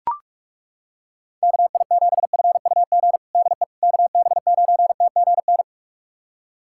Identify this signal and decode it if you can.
Morse code, heard as a keyed tone, about 42 wpm, 700 Hz. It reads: KI7FUG DE KB9TGN
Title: USA Callsign DE Callsign - 6 Characters - 45wpm